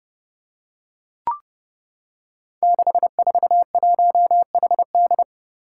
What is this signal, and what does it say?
Morse code, heard as a keyed tone, about 30 wpm, 700 Hz. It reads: 641HD
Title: Five Challenging Letter-Number Combinations 30wpm